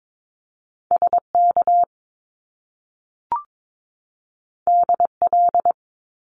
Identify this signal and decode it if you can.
Morse code, heard as a keyed tone, about 22 wpm, 700 Hz. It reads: SK DL
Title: Top Two Letter Combinations in Words 22wpm